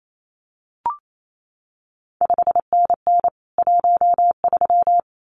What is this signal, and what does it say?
Morse code, heard as a keyed tone, about 28 wpm, 700 Hz. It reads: 5NN 13
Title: CQ WW DX Contest Exchange - 28wpm